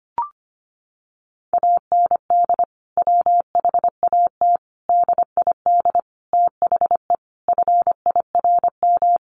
Morse code, heard as a keyed tone, about 25 wpm, 700 Hz. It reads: AND WHAT DID THE FIRM
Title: Sentences from Top 1400 Words 25wpm